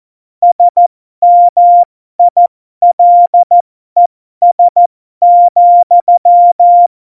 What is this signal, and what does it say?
Morse code, heard as a keyed tone, about 14 wpm, spaced as for 12 wpm, 700 Hz. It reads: SMILES,